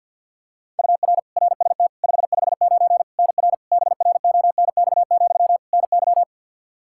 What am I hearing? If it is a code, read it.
Morse code, heard as a keyed tone, about 49 wpm, 700 Hz. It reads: UR RST 559 NR BRONX, NY
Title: K1USN SST QSO Practice 50wpm